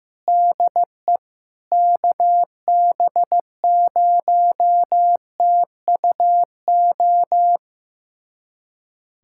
Morse code, heard as a keyed tone, about 16 wpm, 700 Hz. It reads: DE KB0TUO
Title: USA Callsign DE Callsign 15wpm